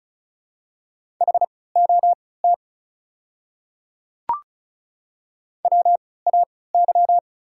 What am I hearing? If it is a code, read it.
Morse code, heard as a keyed tone, about 36 wpm, spaced as for 20 wpm, 690 Hz. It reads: HOT WAY
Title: 3x Character Spacing - Top 100 Words - 35wpm